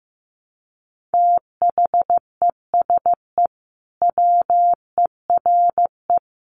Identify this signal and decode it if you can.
Morse code, heard as a keyed tone, about 14 wpm, 700 Hz. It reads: THESE WERE
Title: Sentences from Top 300 Words 15wpm